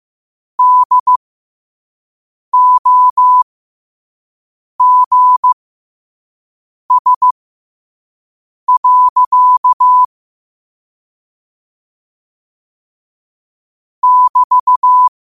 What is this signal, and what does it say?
Morse code, heard as a keyed tone, about 15 wpm, spaced as for 5 wpm, 1000 Hz. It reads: DOGS. <BT>
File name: qotdfiles / 5wpm